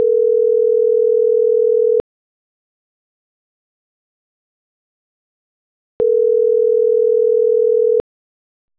Ringback tone (North America), 440 + 480 Hz.